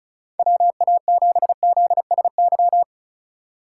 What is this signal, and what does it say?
Morse code, heard as a keyed tone, about 44 wpm, 700 Hz. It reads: WA7ZSY